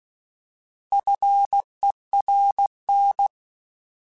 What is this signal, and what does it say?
Morse code, heard as a keyed tone, about 16 wpm, 800 Hz. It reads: FERN
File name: Alphabetical / Alphabetical mp3s / Fs @15